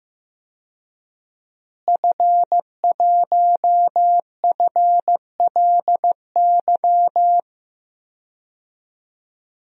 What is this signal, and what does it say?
Morse code, heard as a keyed tone, about 16 wpm, 700 Hz. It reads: F1FLY
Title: DXCC-100 Call Signs - 5 Characters - Repeated 3x - 15wpm